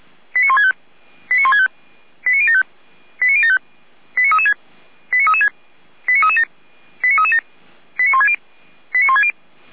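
Five-tone selective call (ZVEI), heard as quick runs of five tones, tones of about 70 ms. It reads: 89165 89165 89075 89075 89336 89336 89337 89337 89160 89160
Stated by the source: ZVEI five-tone call 89168 received on SDR.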